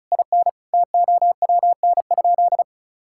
Morse code, heard as a keyed tone, about 36 wpm, 700 Hz. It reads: IN TOWN?
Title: Sentences from Top 1000 Words - Review Part 6 - 35wpm